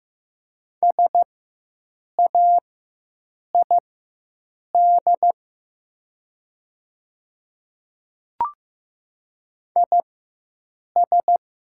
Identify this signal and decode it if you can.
Morse code, heard as a keyed tone, about 15 wpm, spaced as for 7 wpm, 700 Hz. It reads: SAID IS
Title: 4x Character Spacing - Top 300 Words - 15wpm